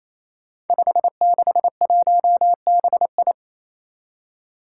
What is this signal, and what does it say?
Morse code, heard as a keyed tone, about 28 wpm, 700 Hz. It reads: 561BI
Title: Five Challenging Letter-Number Combinations 28wpm